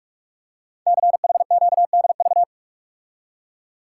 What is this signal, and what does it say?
Morse code, heard as a keyed tone, about 44 wpm, 700 Hz. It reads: CHQDV
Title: Code Groups - 5 Letter:Numbers - 45wpm